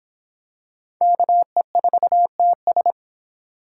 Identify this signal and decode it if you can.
Morse code, heard as a keyed tone, about 25 wpm, 700 Hz. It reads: KE4TS